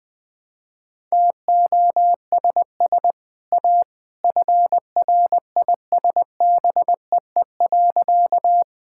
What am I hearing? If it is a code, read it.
Morse code, heard as a keyed tone, about 20 wpm, 700 Hz. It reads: TOSS A FRISBEE.
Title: Easy Sentences with Hints - 20wpm